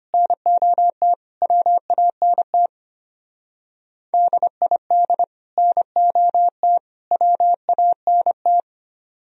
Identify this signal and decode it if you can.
Morse code, heard as a keyed tone, about 32 wpm, 700 Hz. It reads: NOT WANT DID NOT WANT